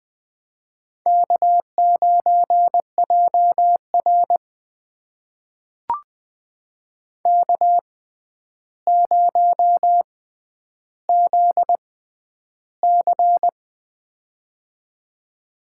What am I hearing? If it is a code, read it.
Morse code, heard as a keyed tone, about 20 wpm, spaced as for 10 wpm, 700 Hz. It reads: K9JR K0ZC
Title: USA Call Signs - 4 Characters - 6x Character Spacing - 20wpm